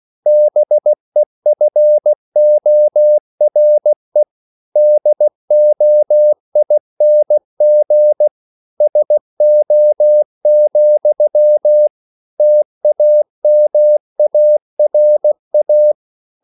Morse code, heard as a keyed tone, about 16 wpm, 600 Hz. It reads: BEFORE DOING SO, TAMARA